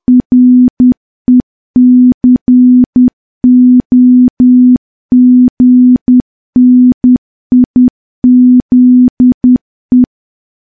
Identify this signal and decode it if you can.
Morse code, heard as a keyed tone, about 10 wpm, 250 Hz. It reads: RECOGNIZE
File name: morse / morse-training-10WPM